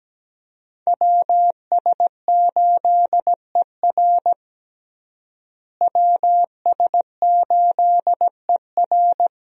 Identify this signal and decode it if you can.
Morse code, heard as a keyed tone, about 17 wpm, 700 Hz. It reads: WS8ER WS8ER
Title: USA Call Signs - 5 Characters - Repeated 3x - 17wpm